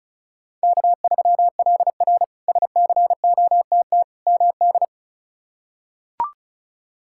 Morse code, heard as a keyed tone, about 36 wpm, 700 Hz. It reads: K3LR SCOTT MD
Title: K1USN SST QSO Practice - 35wpm